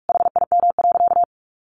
Morse code, heard as a keyed tone, about 44 wpm, 700 Hz. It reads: HIM.
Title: Sentences from Top 900 Words 45wpm